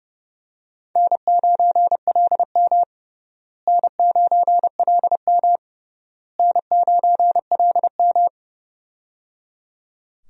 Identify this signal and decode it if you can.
Morse code, heard as a keyed tone, about 30 wpm, 700 Hz. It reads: N9LM N9LM N9LM
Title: USA Call Signs - 4 Characters - Repeated 3x - 30wpm